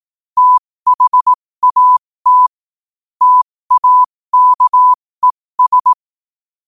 Morse code, heard as a keyed tone, about 18 wpm, spaced as for 15 wpm, 1000 Hz. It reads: THAT TAKES
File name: qotdfiles / 15wpm